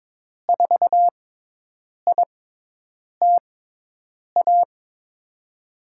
Morse code, heard as a keyed tone, about 22 wpm, spaced as for 8 wpm, 700 Hz. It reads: 4ITA
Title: USA Call Signs - 5 Characters - 6x Character Spacing - 22wpm